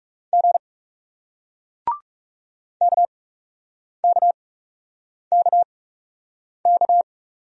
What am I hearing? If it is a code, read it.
Morse code, heard as a keyed tone, about 40 wpm, spaced as for 8 wpm, 700 Hz. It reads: G KKKK